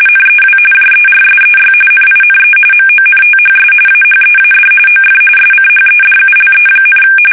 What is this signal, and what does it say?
RTTY, about 45.45 baud. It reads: NEW JERSEY NINE MONTHS AHEAD OF SCHEDULE,
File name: fsk850 / battleshipnj